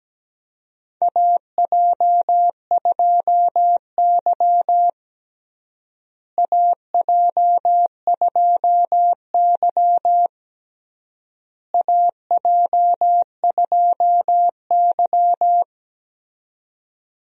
Morse code, heard as a keyed tone, about 17 wpm, 700 Hz. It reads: AJ2Y AJ2Y AJ2Y
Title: USA Call Signs - 4 Characters - Repeated 3x - 17wpm